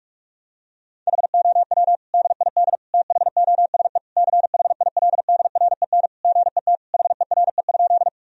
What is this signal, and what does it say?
Morse code, heard as a keyed tone, about 46 wpm, 700 Hz. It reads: HOW DID THOSE CHILDREN GET HERE?
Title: Sentences from Top 1000 Words - Review Part 1 - 45wpm